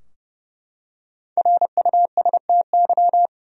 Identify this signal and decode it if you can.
Morse code, heard as a keyed tone, about 30 wpm, 700 Hz. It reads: RUSTY
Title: USA Common Names - Max 5 Letters - 30wpm